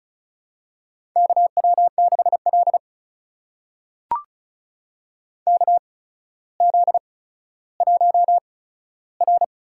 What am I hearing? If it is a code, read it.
Morse code, heard as a keyed tone, about 36 wpm, spaced as for 15 wpm, 700 Hz. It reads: KW6L KZ1R
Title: USA Call Signs - 4 Characters - 8x Character Spacing - 35wpm